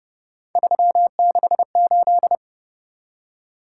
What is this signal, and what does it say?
Morse code, heard as a keyed tone, about 29 wpm, 700 Hz. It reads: 368